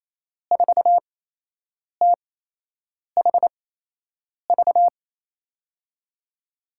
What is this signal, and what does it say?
Morse code, heard as a keyed tone, about 28 wpm, spaced as for 8 wpm, 700 Hz. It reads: 4THV
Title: USA Call Signs - 6 Characters - 8x Character Spacing - 28wpm